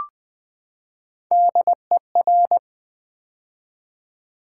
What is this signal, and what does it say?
Morse code, heard as a keyed tone, about 20 wpm, 700 Hz. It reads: DER
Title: Build A Word - 3 Letters At A Time - 6-Letter Words - 20wpm